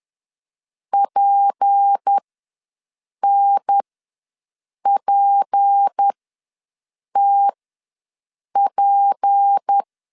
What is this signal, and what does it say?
Morse code, heard as a keyed tone, about 10 wpm, spaced as for 6 wpm, 800 Hz. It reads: PNPTP
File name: LEZIONI / 6 PKX0 / 6 P ws 10wpm 3751